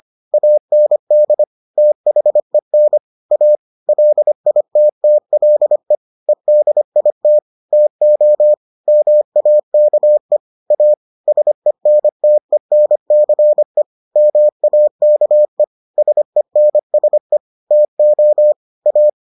Morse code, heard as a keyed tone, about 25 wpm, 600 Hz. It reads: AND THEN A LITTLE EDIT TO MAKE A SENTENCE MAKE SENSE TO A